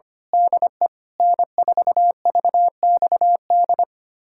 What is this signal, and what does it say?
Morse code, heard as a keyed tone, about 25 wpm, 700 Hz. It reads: DE N4VXD
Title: USA Callsign DE Callsign - 5 Characters - 25wpm